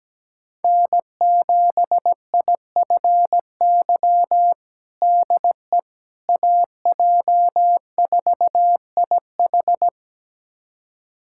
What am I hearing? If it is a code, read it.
Morse code, heard as a keyed tone, about 17 wpm, 700 Hz. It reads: N7IFY DE AJ4IH
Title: USA Callsign DE Callsign - 5 Characters - 17wpm